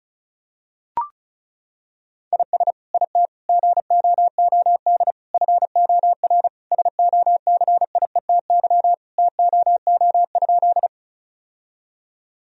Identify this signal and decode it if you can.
Morse code, heard as a keyed tone, about 34 wpm, 700 Hz. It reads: IS IT GOOD FOR SOCIETY TOO?
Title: Sentences from Top 1000 Words - Review Part 4 - 35wpm